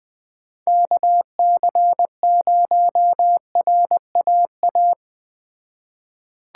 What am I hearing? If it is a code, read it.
Morse code, heard as a keyed tone, about 20 wpm, 700 Hz. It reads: KC0RAA
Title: Speed Racing - USA 2x3 Call Signs 17wpm